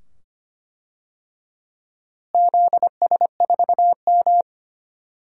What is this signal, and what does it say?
Morse code, heard as a keyed tone, about 25 wpm, 700 Hz. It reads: ZS4M
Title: DXCC-100 Call Signs - 4 Characters - Repeated 2x - 25wpm